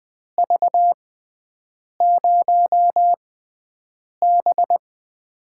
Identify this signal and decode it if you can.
Morse code, heard as a keyed tone, about 20 wpm, spaced as for 7 wpm, 700 Hz. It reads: V0B
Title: USA Call Signs - 4 Characters - 6x Character Spacing - 20wpm